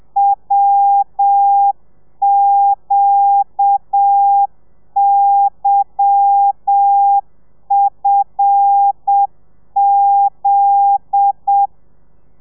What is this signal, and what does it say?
Morse code, heard as a keyed tone, about 7 wpm, 800 Hz. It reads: WQYFZ
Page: Lezione 1: audio a 30 CPM